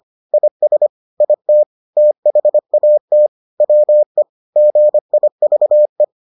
Morse code, heard as a keyed tone, about 25 wpm, 600 Hz. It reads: IS IT THAT WE GIVE